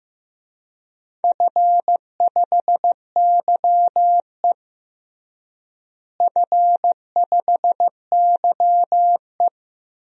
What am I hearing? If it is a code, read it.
Morse code, heard as a keyed tone, about 15 wpm, 700 Hz. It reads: F5YE F5YE
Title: DXCC-100 Call Signs - 4 Characters - Repeated 3x - 15wpm